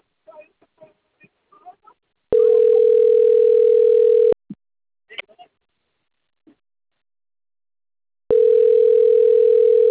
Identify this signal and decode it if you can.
Ringback tone (North America), 440 + 480 Hz.
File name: out- - - - - .87966